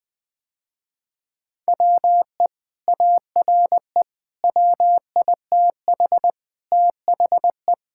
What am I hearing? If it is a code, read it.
Morse code, heard as a keyed tone, about 20 wpm, 700 Hz. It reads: WE ARE WITH THE
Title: Sentences with 2-Word Phrases - Spoken Beginning - 20wpm